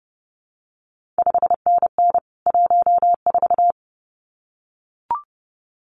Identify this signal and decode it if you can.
Morse code, heard as a keyed tone, about 30 wpm, 700 Hz. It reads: 5NN 14
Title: CQ WW DX Contest Exchange - 30wpm